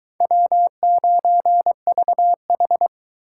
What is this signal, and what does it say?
Morse code, heard as a keyed tone, about 22 wpm, 700 Hz. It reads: W9VH